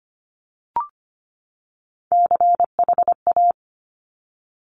Morse code, heard as a keyed tone, about 25 wpm, 700 Hz. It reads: CHA
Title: Top Three Letter Combinations in Words 25wpm